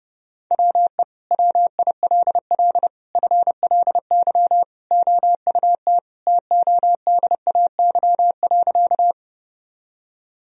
Morse code, heard as a keyed tone, about 32 wpm, 700 Hz. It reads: WE WILL FLY OUT TODAY.
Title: Sentences from Top 1000 Words - Review Part 2 - 30wpm